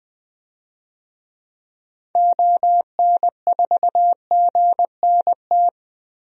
Morse code, heard as a keyed tone, about 20 wpm, 700 Hz. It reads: ON4GNT
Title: DXCC-100 Call Signs - 6 Characters - 20wpm